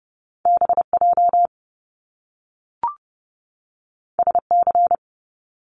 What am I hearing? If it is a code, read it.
Morse code, heard as a keyed tone, about 30 wpm, 700 Hz. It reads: BJ SC
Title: Top Two Letter Combinations in Words 30wpm